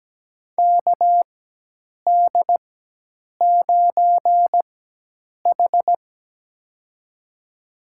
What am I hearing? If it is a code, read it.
Morse code, heard as a keyed tone, about 17 wpm, spaced as for 8 wpm, 700 Hz. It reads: KD9H
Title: USA Call Signs - 4 Characters - 4x Character Spacing - 17wpm